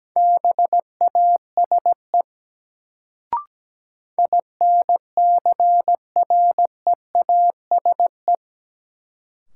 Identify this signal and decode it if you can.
Morse code, heard as a keyed tone, about 17 wpm, 700 Hz. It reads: BASE INCREASE
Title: Sets of 5 Words - New Word Drill - 17wpm